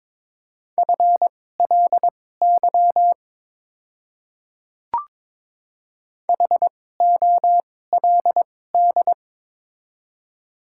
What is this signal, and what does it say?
Morse code, heard as a keyed tone, about 22 wpm, spaced as for 16 wpm, 700 Hz. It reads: FLY HOLD
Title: 2x Character Spacing - Top 500 Words - 22wpm